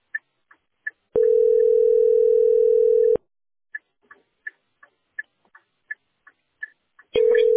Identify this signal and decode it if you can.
Ringback tone (North America), 440 + 480 Hz.